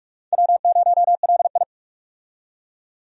Morse code, heard as a keyed tone, about 46 wpm, 700 Hz. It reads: W0LI
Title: USA Call Signs - 4 Characters - 45wpm